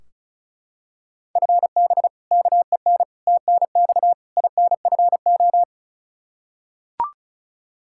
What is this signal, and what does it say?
Morse code, heard as a keyed tone, about 36 wpm, 700 Hz. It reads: FB KEN TNX INFO
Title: Practice QSO - Exchange 3 - More Info - 35wpm